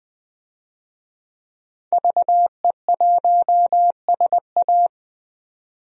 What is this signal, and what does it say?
Morse code, heard as a keyed tone, about 21 wpm, 700 Hz. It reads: VE1SA
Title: DXCC-100 Call Signs - 5 Characters - 20wpm